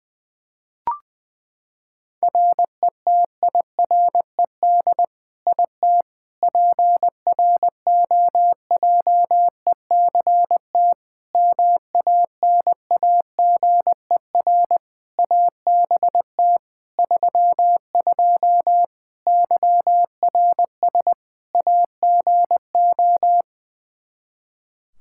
Morse code, heard as a keyed tone, about 21 wpm, 700 Hz. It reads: RETIRED IT PROJECT MANAGER ABT 32 YRS AGO
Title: Practice QSO - Exchange 3 - More Info - 20wpm